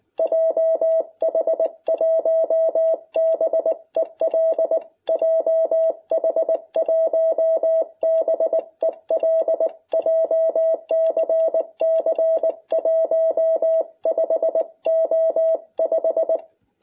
Morse code, heard as a keyed tone, about 20 wpm, 660 Hz. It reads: JH1BEL JH1BEL JCC 15O5